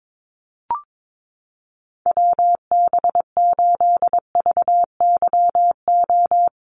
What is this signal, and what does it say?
Morse code, heard as a keyed tone, about 22 wpm, 700 Hz. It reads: WB8VYO
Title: USA Call Signs Repeated 3x 22wpm